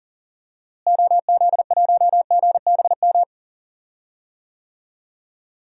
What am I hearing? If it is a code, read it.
Morse code, heard as a keyed tone, about 40 wpm, 700 Hz. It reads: OZ1GBM
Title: DXCC-100 Call Signs - 6 Characters - Repeated 3x - 40wpm